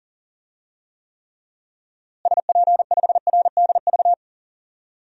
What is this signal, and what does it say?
Morse code, heard as a keyed tone, about 37 wpm, 700 Hz. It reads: SP5RDV
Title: DXCC-100 Call Signs - 6 Characters - Repeated 3x - 40wpm